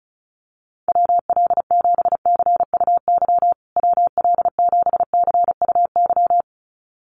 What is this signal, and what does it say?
Morse code, heard as a keyed tone, about 34 wpm, 700 Hz. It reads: WL7CUY WL7CUY
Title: USA Call Signs Repeated 2x 35wpm